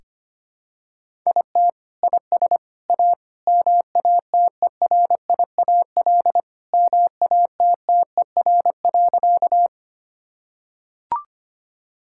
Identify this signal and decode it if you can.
Morse code, heard as a keyed tone, about 25 wpm, 700 Hz. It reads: IT IS A MATERIAL MATTER.
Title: Sentences from Top 1000 Words - Review Part 5 - 25wpm